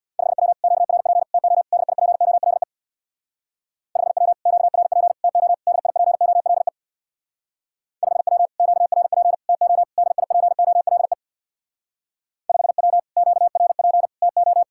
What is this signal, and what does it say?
Morse code, heard as a keyed tone, about 67 wpm, 700 Hz. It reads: 5W QRP TO DIPOLE 5W QRP TO DIPOLE 5W QRP TO DIPOLE 5W QRP TO